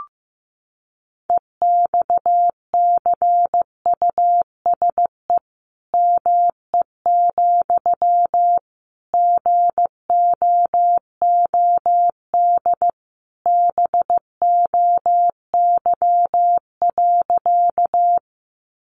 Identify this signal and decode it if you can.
Morse code, heard as a keyed tone, about 15 wpm, 700 Hz. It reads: EXCUSE ME, GOOD BOY.
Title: Sentences from Top 700 Words 15wpm